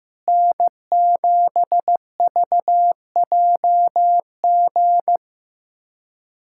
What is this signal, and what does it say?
Morse code, heard as a keyed tone, about 16 wpm, 700 Hz. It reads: N7VJG
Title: USA Callsign DE Callsign - 5 Characters - 15wpm